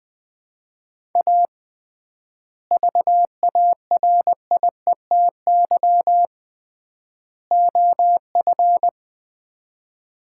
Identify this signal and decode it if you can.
Morse code, heard as a keyed tone, about 20 wpm, 700 Hz. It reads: A VARIETY OF
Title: 3x Word-Spacing - Sets of 3 Words - 20wpm